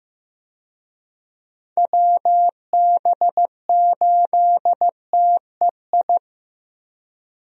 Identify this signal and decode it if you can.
Morse code, heard as a keyed tone, about 15 wpm, 700 Hz. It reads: WB8TEI
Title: USA Call Signs - 6 Characters - Repeated 2x - 15wpm